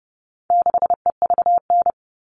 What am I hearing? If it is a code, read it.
Morse code, heard as a keyed tone, about 30 wpm, 700 Hz. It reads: 6EVN
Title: All Cut Numbers - 4-Digit 30wpm